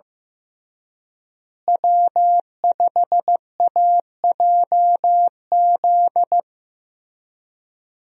Morse code, heard as a keyed tone, about 15 wpm, 700 Hz. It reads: W5AJZ
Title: USA Call Signs - 5 Characters - Repeated 3x - 15wpm